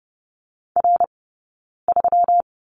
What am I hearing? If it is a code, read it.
Morse code, heard as a keyed tone, about 30 wpm, 700 Hz. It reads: R 3
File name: Lesson 20d - Number 3 - Two Characters 30wpm